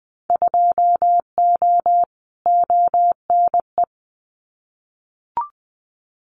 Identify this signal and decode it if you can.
Morse code, heard as a keyed tone, about 19 wpm, 700 Hz. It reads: 2O ONE
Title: Winter Field Day Exchanges Encore 20wpm